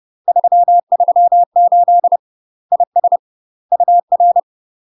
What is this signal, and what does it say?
Morse code, heard as a keyed tone, about 30 wpm, 700 Hz. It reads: 338 IS UR
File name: UK MASTER QSOs TRAINING / 30WPM / UK MASTER QSOs 0035